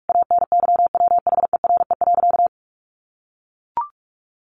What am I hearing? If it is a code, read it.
Morse code, heard as a keyed tone, about 44 wpm, 700 Hz. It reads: ANYWHERE.
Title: Sentences from Top 1500 Words 45wpm